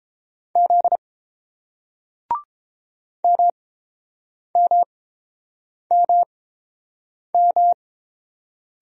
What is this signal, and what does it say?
Morse code, heard as a keyed tone, about 31 wpm, spaced as for 7 wpm, 700 Hz. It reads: Z MMMM